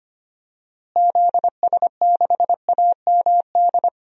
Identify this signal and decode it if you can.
Morse code, heard as a keyed tone, about 25 wpm, 700 Hz. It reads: ZS6AMD
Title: DXCC-100 Call Signs - 6 Characters - Repeated 2x - 25wpm